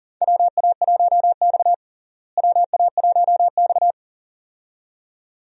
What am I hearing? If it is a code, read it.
Morse code, heard as a keyed tone, about 40 wpm, 700 Hz. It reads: WA1X WA1X
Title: USA Call Signs - 4 Characters - Repeated 2x - 40wpm